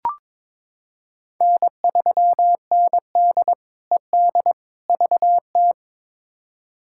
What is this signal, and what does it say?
Morse code, heard as a keyed tone, about 22 wpm, 700 Hz. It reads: N3ND ED VT
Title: K1USN SST QSO Practice - 22wpm